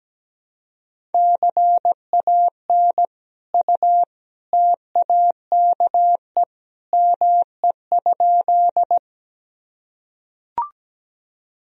Morse code, heard as a keyed tone, about 17 wpm, 700 Hz. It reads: CAN U TAKE ME?
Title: Sentences from Top 200 Words and Abbreviations 17wpm